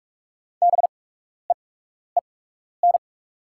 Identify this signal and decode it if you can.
Morse code, heard as a keyed tone, about 44 wpm, spaced as for 12 wpm, 700 Hz. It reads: BEEN
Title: 8x Character Spacing - Top 300 Words - 45wpm